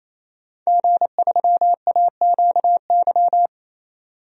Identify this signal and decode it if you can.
Morse code, heard as a keyed tone, about 28 wpm, 700 Hz. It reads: G3AQY
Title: DXCC-100 Call Signs - 5 Characters - 28wpm